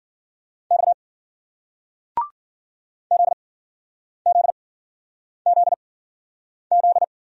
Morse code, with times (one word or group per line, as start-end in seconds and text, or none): 0.7-0.9 X
3.1-7.0 ZZZZ